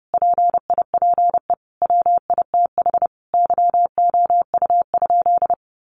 Morse code, heard as a keyed tone, about 28 wpm, 700 Hz. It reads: PIPE WITH YOU?
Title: Sentences from Top 1500 Words 30wpm